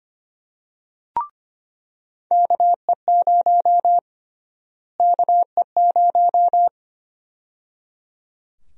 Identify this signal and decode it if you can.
Morse code, heard as a keyed tone, about 25 wpm, 700 Hz. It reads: KE0 KE0
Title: USA Call Sign Prefixes - Repeated 2x - 25wpm